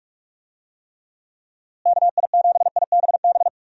Morse code, heard as a keyed tone, about 47 wpm, 700 Hz. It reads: KI7IBB
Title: USA Call Signs - 6 Characters - Repeated 2x - 45wpm